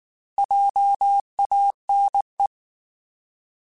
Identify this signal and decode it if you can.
Morse code, heard as a keyed tone, about 19 wpm, 800 Hz. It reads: JANE
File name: Alphabetical / Alphabetical mp3s / Js @18